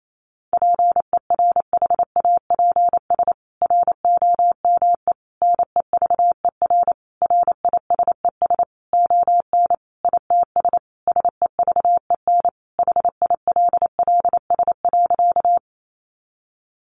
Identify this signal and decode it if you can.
Morse code, heard as a keyed tone, about 28 wpm, 700 Hz. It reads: PERHAPS ROME NEVER RISES ON ITS SEVEN HILLS.
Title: A Sound Of Thunder 28wpm